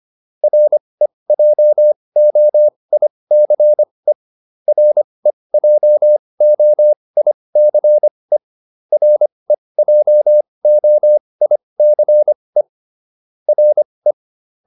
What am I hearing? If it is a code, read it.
Morse code, heard as a keyed tone, about 25 wpm, spaced as for 20 wpm, 600 Hz. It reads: REJOICE REJOICE REJOICE RE
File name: RE7 20